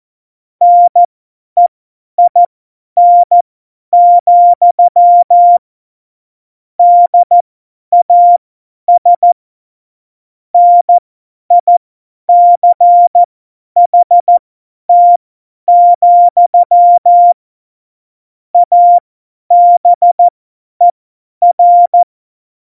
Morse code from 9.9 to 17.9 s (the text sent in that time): NICHT,